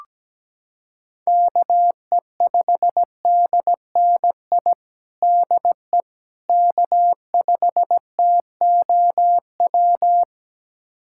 Morse code, heard as a keyed tone, about 17 wpm, 700 Hz. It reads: KE5DNI DE K5TOW
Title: USA Callsign DE Callsign 17wpm